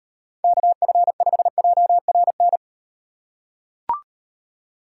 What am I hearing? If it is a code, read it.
Morse code, heard as a keyed tone, about 38 wpm, 700 Hz. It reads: KF5JRN